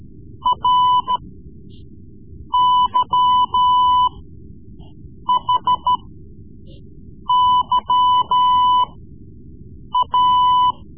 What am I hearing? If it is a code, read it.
Morse code, heard as a keyed tone, about 13 wpm, spaced as for 6 wpm, 1000 Hz. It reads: RYHYA
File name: Morse Lessons / Morse Lessons / Tape-3B